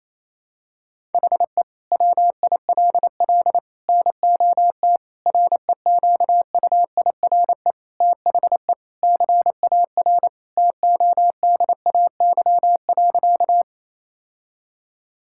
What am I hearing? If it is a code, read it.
Morse code, heard as a keyed tone, about 28 wpm, 700 Hz. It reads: HE WILL NOT REQUIRE THE CAR TODAY.
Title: Sentences from Top 1000 Words - Review Part 6 - 28wpm